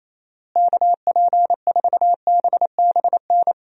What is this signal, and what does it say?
Morse code, heard as a keyed tone, about 28 wpm, 700 Hz. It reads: KP4BBN
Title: DXCC-100 Call Signs - 6 Characters - Repeated 2x - 28wpm